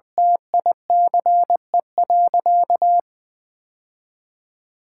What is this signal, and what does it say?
Morse code, heard as a keyed tone, about 21 wpm, 700 Hz. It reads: TICE.
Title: Sentences from Top 1000 Words - Review Part 5 - 20wpm